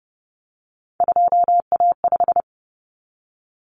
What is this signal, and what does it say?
Morse code, heard as a keyed tone, about 30 wpm, 700 Hz. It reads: 2A5
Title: Common Cut Numbers - 3-Digit 30wpm